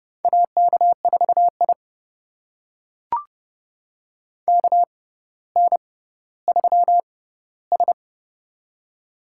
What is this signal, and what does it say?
Morse code, heard as a keyed tone, about 30 wpm, spaced as for 15 wpm, 700 Hz. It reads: AK4I KN3S
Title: USA Call Signs - 4 Characters - 6x Character Spacing - 30wpm